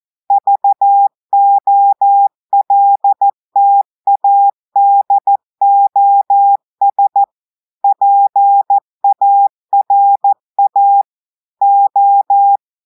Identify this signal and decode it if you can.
Morse code, heard as a keyed tone, about 14 wpm, 800 Hz. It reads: VOLTADOS PARA O
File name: audios / portugais / portugais texte en clair 2 vitesse 14 mots minutes